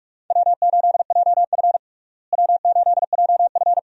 Morse code, heard as a keyed tone, about 46 wpm, 700 Hz. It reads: W8JF W8JF
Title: USA Call Signs - 4 Characters - Repeated 2x - 45wpm